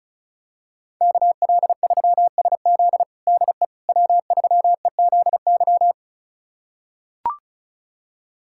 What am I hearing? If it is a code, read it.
Morse code, heard as a keyed tone, about 34 wpm, 700 Hz. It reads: KL3SZ DE W3EZY
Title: USA Callsign DE Callsign - 5 Characters - 35wpm